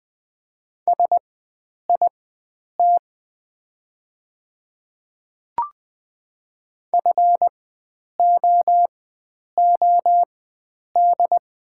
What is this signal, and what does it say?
Morse code, heard as a keyed tone, about 20 wpm, spaced as for 9 wpm, 700 Hz. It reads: SIT FOOD
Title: 4x Character Spacing - Top 500 Words - 20wpm